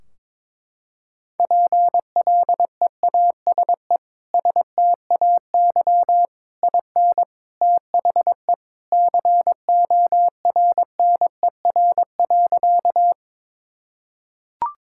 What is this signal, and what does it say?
Morse code, heard as a keyed tone, about 22 wpm, 700 Hz. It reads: PLEASE STAY IN THE CORNER.
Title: Sentences from Top 1600 Words 22wpm